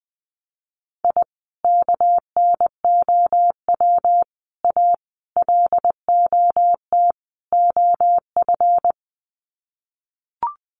Morse code, heard as a keyed tone, about 20 wpm, 700 Hz. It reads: I KNOW A LOT OF
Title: Sets of 5 Words Encore 20wpm